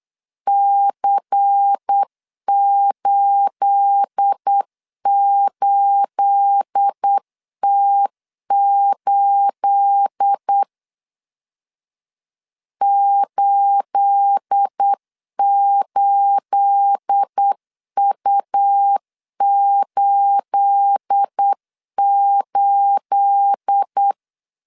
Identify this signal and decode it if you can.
Morse code, heard as a keyed tone, about 8 wpm, 800 Hz. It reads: C88T8 88U88
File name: LEZIONI / 9 28 8wpm 3512